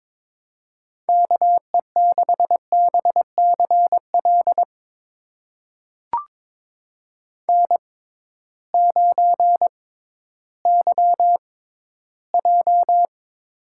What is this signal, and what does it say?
Morse code, heard as a keyed tone, about 22 wpm, spaced as for 13 wpm, 700 Hz. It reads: KE6BCL N9YJ
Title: USA Call Signs - 6x Character Spacing - 22wpm